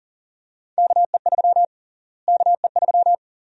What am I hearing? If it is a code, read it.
Morse code, heard as a keyed tone, about 40 wpm, 700 Hz. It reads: KE3 KE3
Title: USA Call Sign Prefixes - Repeated 2x - 40wpm